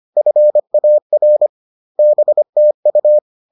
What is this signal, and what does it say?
Morse code, heard as a keyed tone, about 25 wpm, 600 Hz. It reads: FAR BTU